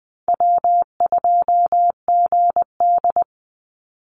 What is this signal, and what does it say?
Morse code, heard as a keyed tone, about 20 wpm, 700 Hz. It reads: W2GD
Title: Call Signs - 20wpm